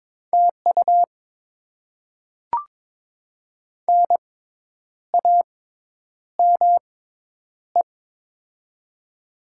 Morse code, heard as a keyed tone, about 23 wpm, spaced as for 9 wpm, 700 Hz. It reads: TU NAME
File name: Top 20 QSO Elements - 6x Character Spacing - 22wpm